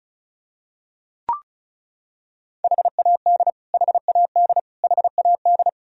Morse code, heard as a keyed tone, about 36 wpm, 700 Hz. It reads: HAD HAD HAD
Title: Top 100 Words Repeated 35wpm